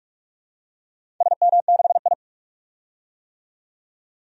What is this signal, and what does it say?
Morse code, heard as a keyed tone, about 46 wpm, 700 Hz. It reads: SM6I
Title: DXCC-100 Call Signs - 4 Characters - 45wpm